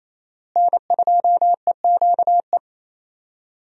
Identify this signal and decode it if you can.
Morse code, heard as a keyed tone, about 27 wpm, 700 Hz. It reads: N2EQE
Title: USA Call Signs - 5 Characters - Repeated 2x - 28wpm